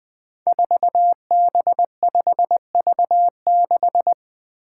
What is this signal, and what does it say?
Morse code, heard as a keyed tone, about 20 wpm, 700 Hz. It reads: 4B5V6
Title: Five Challenging Letter-Number Combinations 20wpm